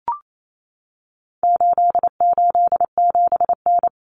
Morse code, heard as a keyed tone, about 28 wpm, 700 Hz. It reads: 887N
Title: Common Cut Numbers - 4-Digit 28wpm